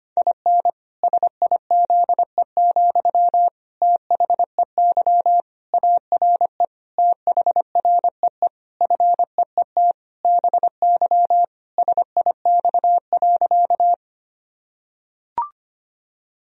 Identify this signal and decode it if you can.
Morse code, heard as a keyed tone, about 25 wpm, 700 Hz. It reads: IN SIZE, THEY ARE THREE FEET BY SIX.
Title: Sentences from Top 1000 Words - Review Part 2 - 25wpm